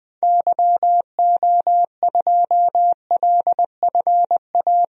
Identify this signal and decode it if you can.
Morse code, heard as a keyed tone, about 21 wpm, 700 Hz. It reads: YO2LFA
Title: DXCC-100 Call Signs - 6 Characters - Repeated 2x - 20wpm